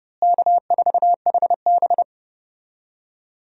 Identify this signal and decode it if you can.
Morse code, heard as a keyed tone, about 32 wpm, 700 Hz. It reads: K4HB
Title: USA Call Signs - 4 Characters - Repeated 3x - 30wpm